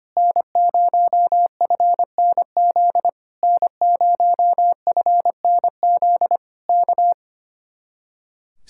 Morse code, heard as a keyed tone, about 24 wpm, 700 Hz. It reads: N0FNZ N0FNZ K
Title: Speed-Racing Common QSO Phrases - 25wpm